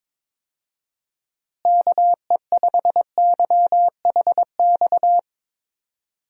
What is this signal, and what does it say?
Morse code, heard as a keyed tone, about 22 wpm, 700 Hz. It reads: KE5YHX
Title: USA Call Signs - 6 Characters - Repeated 3x - 22wpm